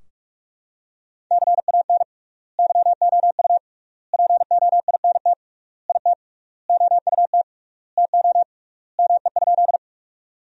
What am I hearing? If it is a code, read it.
Morse code, heard as a keyed tone, about 46 wpm, 700 Hz. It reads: CAN YOU POINT IT OUT TO ME?
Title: 3x Word-Spacing - Sentences from Top 200 Words - 45wpm